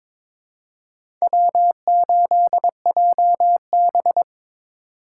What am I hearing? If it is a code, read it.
Morse code, heard as a keyed tone, about 22 wpm, 700 Hz. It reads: W8JB
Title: USA Call Signs - 4 Characters - 2x Character Spacing - 22wpm